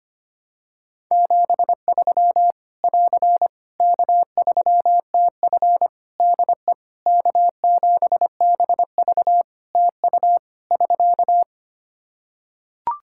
Morse code, heard as a keyed tone, about 25 wpm, 700 Hz. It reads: 73 <AR> K3TF DE K7BV TU <SK>
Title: Practice QSO - Exchange 4 - Ending - 25wpm